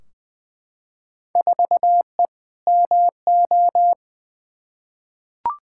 Morse code, heard as a keyed tone, about 20 wpm, 700 Hz. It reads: 4E MO
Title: Field Day Exchanges Encore - 20wpm